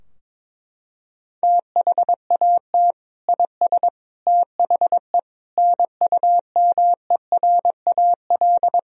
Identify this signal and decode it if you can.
Morse code, heard as a keyed tone, about 22 wpm, 700 Hz. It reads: THAT IS THE NUMERAL
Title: Sentences from Top 1000 Words - Review Part 2 - 22wpm